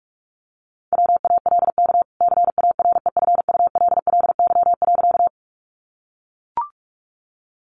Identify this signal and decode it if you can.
Morse code, heard as a keyed tone, about 44 wpm, 700 Hz. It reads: WALK CAREFULLY.
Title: Sentences from Top 1300 Words 45wpm